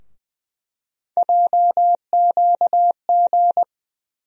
Morse code, heard as a keyed tone, about 21 wpm, 700 Hz. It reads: JQG
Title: USA Call Sign Postfixes - 3 Letters 20wpm